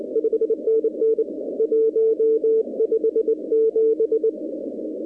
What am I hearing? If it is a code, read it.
Morse code, heard as a keyed tone, about 23 wpm, 450 Hz. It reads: 5NN 157
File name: contests / 2019-cw-wpx-kt5j-k5pi- -2330 0-41